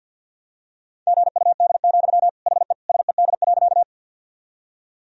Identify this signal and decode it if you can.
Morse code, heard as a keyed tone, about 50 wpm, 700 Hz. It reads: GUD, HE SED.
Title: Sentences from Top 200 Words and Abbreviations 50wpm